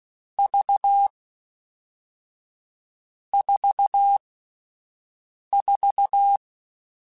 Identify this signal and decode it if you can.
Morse code, heard as a keyed tone, about 16 wpm, 800 Hz. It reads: V 4 4